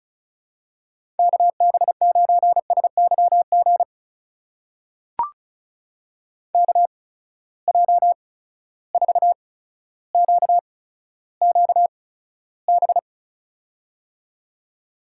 Morse code, heard as a keyed tone, about 34 wpm, spaced as for 15 wpm, 700 Hz. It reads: KB9SYG KJ4QQB
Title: USA Call Signs - 6 Characters - 8x Character Spacing - 35wpm